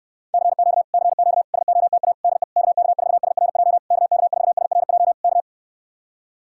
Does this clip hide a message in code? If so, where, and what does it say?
Yes: Morse code from 0.3 to 5.4 s, heard as a keyed tone, about 64 wpm, 700 Hz. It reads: CQ CQ SOTA DE KK4SRJ KK4SRJ K